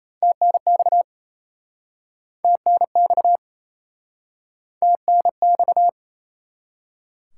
Morse code, heard as a keyed tone, about 37 wpm, 700 Hz. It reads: TNX TNX TNX